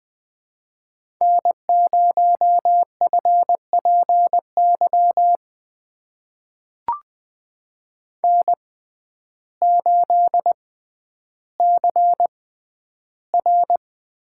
Morse code, heard as a keyed tone, about 20 wpm, spaced as for 11 wpm, 700 Hz. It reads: N0FPY N8CR
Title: USA Call Signs - 5 Characters - 6x Character Spacing - 20wpm